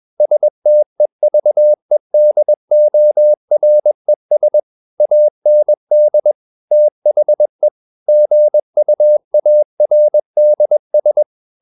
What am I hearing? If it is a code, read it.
Morse code, heard as a keyed tone, about 21 wpm, 600 Hz. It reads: STEVEDORES AND THE GUARDS
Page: CW med Gnister